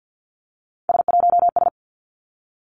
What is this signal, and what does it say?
Morse code, heard as a keyed tone, about 50 wpm, 700 Hz. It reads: S1S
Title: Three Challenging Letter-Number Combinations 50wpm